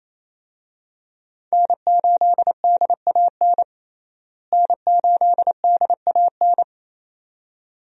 Morse code, heard as a keyed tone, about 28 wpm, 700 Hz. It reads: N8DAN N8DAN
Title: USA Call Signs - 5 Characters - Repeated 2x - 28wpm